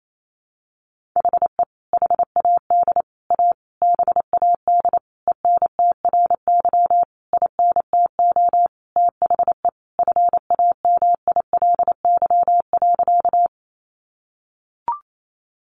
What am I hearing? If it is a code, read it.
Morse code, heard as a keyed tone, about 28 wpm, 700 Hz. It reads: HE HAD A BAD ENTRY INTO THE FAMILY.
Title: Sentences from Top 1500 Words 28wpm